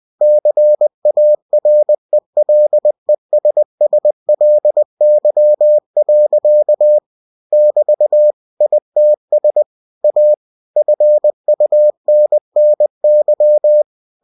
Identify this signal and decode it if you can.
Morse code, heard as a keyed tone, about 20 wpm, 600 Hz. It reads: CARELESSLY. <BT> ITS A FUNNY
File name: cw-books / winnie-the-pooh / winnie-20wpm-page-0084